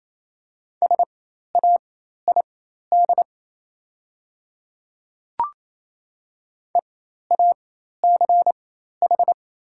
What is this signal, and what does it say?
Morse code, heard as a keyed tone, about 27 wpm, spaced as for 13 wpm, 700 Hz. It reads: SAID EACH
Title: 4x Character Spacing - Top 200 Words - 28wpm